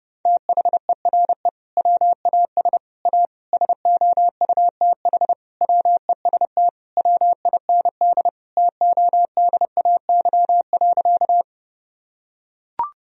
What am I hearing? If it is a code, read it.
Morse code, heard as a keyed tone, about 30 wpm, 700 Hz. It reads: THERE WAS A SOUTH WEST WIND TODAY.
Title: Sentences from Top 1000 Words - Review Part 2 - 30wpm